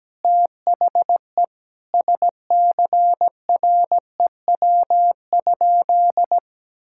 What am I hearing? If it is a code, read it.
Morse code, heard as a keyed tone, about 17 wpm, 700 Hz. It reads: THE SCREW?
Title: Sentences from Top 1600 Words 17wpm